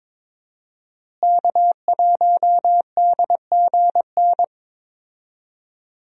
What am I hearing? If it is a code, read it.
Morse code, heard as a keyed tone, about 22 wpm, 700 Hz. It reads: K1DGN
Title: USA Call Signs - 5 Characters - Repeated 2x - 22wpm